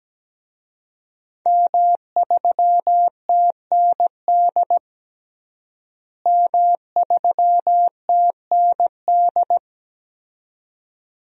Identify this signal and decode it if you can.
Morse code, heard as a keyed tone, about 17 wpm, 700 Hz. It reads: M3TND M3TND
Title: DXCC-100 Call Signs - 5 Characters - Repeated 2x - 17wpm